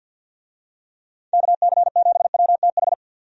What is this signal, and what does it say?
Morse code, heard as a keyed tone, about 52 wpm, 700 Hz. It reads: KC7WTH
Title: USA Call Signs - 6 Characters - 50wpm